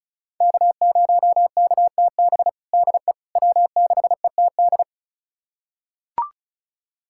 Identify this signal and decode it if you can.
Morse code, heard as a keyed tone, about 34 wpm, 700 Hz. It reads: K0KTB DE W6ETD
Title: USA Callsign DE Callsign - 5 Characters - 35wpm